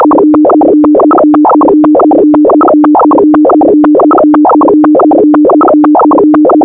Two-tone teleprinter signal, about 200 Hz shift.
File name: audio notizie / radia s24 n297 radiohelsinki raggam dataradio pt4